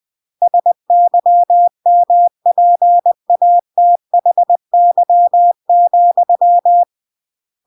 Morse code, heard as a keyed tone, about 20 wpm, 700 Hz. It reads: SYMPATHY,
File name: CW / jodorowsky / 82 commandments 20-20